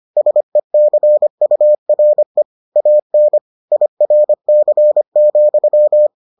Morse code, heard as a keyed tone, about 25 wpm, 600 Hz. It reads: SECURE AN IRC,